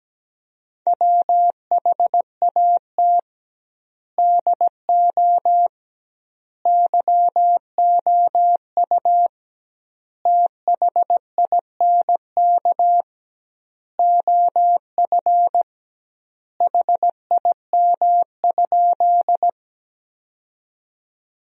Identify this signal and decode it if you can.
Morse code, heard as a keyed tone, about 17 wpm, 700 Hz. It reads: WHAT DO YOU THINK OF HIM?
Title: 2x Word-Spacing - Sentences from Top 200 Words - 17wpm